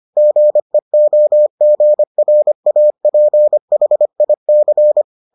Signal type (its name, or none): Morse code